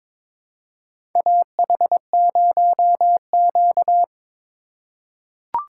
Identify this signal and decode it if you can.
Morse code, heard as a keyed tone, about 22 wpm, 700 Hz. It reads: AH0Q
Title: USA Call Signs - 4 Characters - 3x Character Spacing - 22wpm